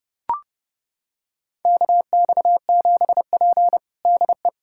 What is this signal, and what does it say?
Morse code, heard as a keyed tone, about 30 wpm, 700 Hz. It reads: KX7P DE
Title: USA Callsign DE Callsign - 4 Characters - 30wpm